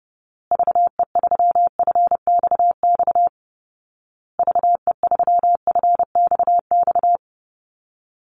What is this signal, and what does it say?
Morse code, heard as a keyed tone, about 30 wpm, 700 Hz. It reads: VE3FXX VE3FXX
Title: Call Signs Repeated 2x - 30wpm